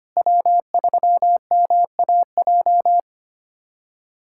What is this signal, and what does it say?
Morse code, heard as a keyed tone, about 25 wpm, 700 Hz. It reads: W3MAJ
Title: USA Callsign DE Callsign - 5 Characters - 25wpm